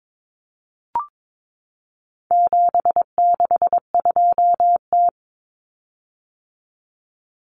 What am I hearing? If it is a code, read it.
Morse code, heard as a keyed tone, about 22 wpm, 700 Hz. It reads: 762T
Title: Common Cut Numbers - 4-Digit 22wpm